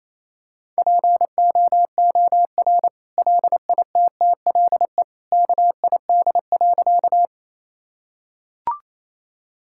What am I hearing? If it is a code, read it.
Morse code, heard as a keyed tone, about 28 wpm, 700 Hz. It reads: POOR LITTLE KID.
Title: Sentences from Top 1600 Words 28wpm